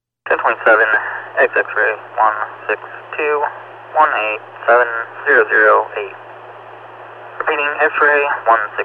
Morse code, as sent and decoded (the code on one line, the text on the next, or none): none